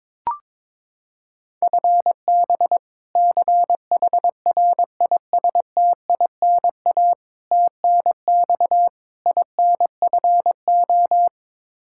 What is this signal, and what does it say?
Morse code, heard as a keyed tone, about 21 wpm, 700 Hz. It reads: FB CHRISTINA TNX INFO
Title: Practice QSO - Exchange 3 - More Info - 22wpm